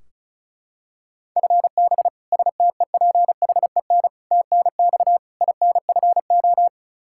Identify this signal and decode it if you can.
Morse code, heard as a keyed tone, about 34 wpm, 700 Hz. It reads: FB STEPHEN TNX INFO
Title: Practice QSO - Exchange 3 - More Info - 35wpm